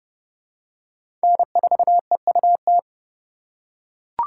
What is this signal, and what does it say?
Morse code, heard as a keyed tone, about 30 wpm, 700 Hz. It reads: N4EUT
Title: USA Call Signs - 4x Character Spacing - 30wpm